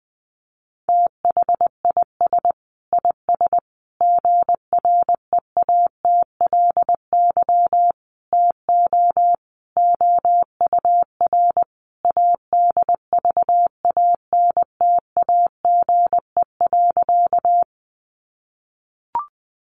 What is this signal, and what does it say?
Morse code, heard as a keyed tone, about 20 wpm, 700 Hz. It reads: THIS IS GREATLY TO OUR ADVANTAGE.
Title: Sentences from Top 1400 Words 20wpm